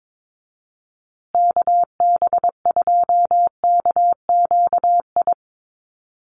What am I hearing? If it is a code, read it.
Morse code, heard as a keyed tone, about 22 wpm, 700 Hz. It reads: KB2KQI
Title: USA Call Signs Repeated 3x 22wpm